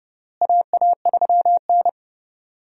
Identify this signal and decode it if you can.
Morse code, heard as a keyed tone, about 30 wpm, 700 Hz. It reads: AA3N
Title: USA Call Signs - 4 Characters - Repeated 2x - 30wpm